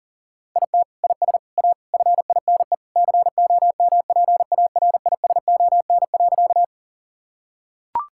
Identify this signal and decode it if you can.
Morse code, heard as a keyed tone, about 37 wpm, 700 Hz. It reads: IT IS A FINE COMPARISON.
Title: Sentences from Top 1600 Words 40wpm